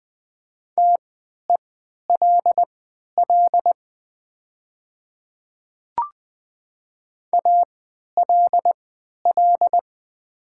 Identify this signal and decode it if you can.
Morse code, heard as a keyed tone, about 20 wpm, spaced as for 11 wpm, 700 Hz. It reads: TELL ALL
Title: 3x Character Spacing - Top 500 Words - 20wpm